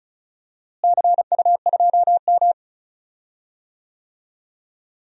Morse code, heard as a keyed tone, about 36 wpm, 700 Hz. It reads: CU2M
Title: DXCC-100 Call Signs - 4 Characters - Repeated 2x - 35wpm